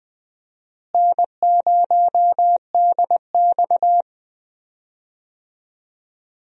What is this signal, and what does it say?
Morse code, heard as a keyed tone, about 20 wpm, 700 Hz. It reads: N0DX
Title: USA Call Signs - 4 Characters - Repeated 2x - 20wpm